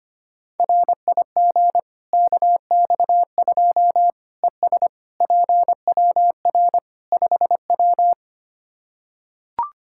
Morse code, heard as a keyed tone, about 25 wpm, 700 Hz. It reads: RIG KX2 ES PWR 5W
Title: Practice QSO - Exchange 2 - Info - 25wpm